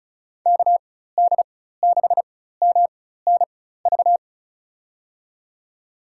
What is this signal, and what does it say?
Morse code, heard as a keyed tone, about 36 wpm, spaced as for 16 wpm, 700 Hz. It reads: KD6MNV
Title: USA Call Signs - 6 Characters - 4x Character Spacing - 35wpm